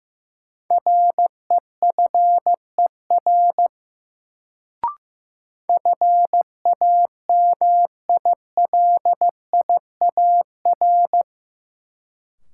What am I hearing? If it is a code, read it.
Morse code, heard as a keyed tone, about 15 wpm, 700 Hz. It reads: REFER FAMILIAR
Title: Sentences from Top 1000 Words - New Word Drill - 15wpm